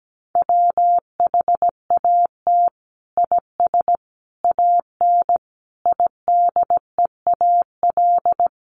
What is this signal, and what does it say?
Morse code, heard as a keyed tone, about 17 wpm, 700 Hz. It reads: WHAT IS AN IDEAL
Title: Sentences from Top 1400 Words 17wpm